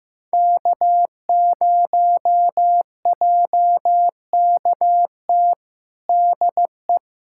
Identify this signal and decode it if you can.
Morse code, heard as a keyed tone, about 15 wpm, 700 Hz. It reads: K0JKT DE
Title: USA Callsign DE Callsign 15wpm